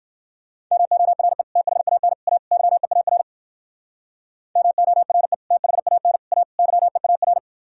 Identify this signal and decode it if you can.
Morse code, heard as a keyed tone, about 51 wpm, 700 Hz. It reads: MORE THAN A YEAR MORE THAN A YEAR
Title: Speed-Racing - Sets of 4 Words 45wpm